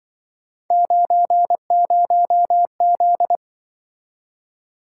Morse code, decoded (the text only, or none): 90Z